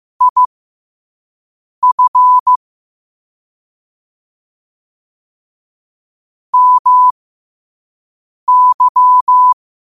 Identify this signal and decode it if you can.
Morse code, heard as a keyed tone, about 15 wpm, spaced as for 5 wpm, 1000 Hz. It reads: IF MY
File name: qotdfiles / 5wpm